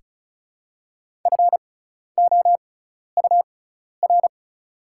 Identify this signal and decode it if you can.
Morse code, heard as a keyed tone, about 36 wpm, spaced as for 12 wpm, 700 Hz. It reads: FOUR
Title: 6x Character Spacing - Top 300 Words - 35wpm